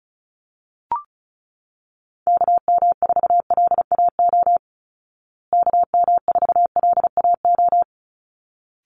Morse code, heard as a keyed tone, about 34 wpm, 700 Hz. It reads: KM4LAO KM4LAO
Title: Call Signs Repeated 3x - 35wpm